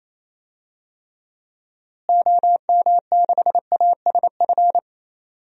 Morse code, heard as a keyed tone, about 28 wpm, 700 Hz. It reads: OM6ASF
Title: DXCC-100 Call Signs - 6 Characters - Repeated 3x - 28wpm